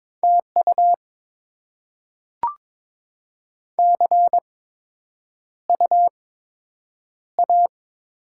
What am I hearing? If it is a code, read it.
Morse code, heard as a keyed tone, about 22 wpm, spaced as for 9 wpm, 700 Hz. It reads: TU CU A
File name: Top 20 QSO Elements - 8x Character Spacing - 22wpm